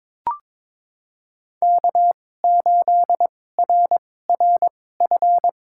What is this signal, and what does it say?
Morse code, heard as a keyed tone, about 22 wpm, spaced as for 16 wpm, 700 Hz. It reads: K8RRF
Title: USA Call Signs - 5 Characters - 2x Character Spacing - 22wpm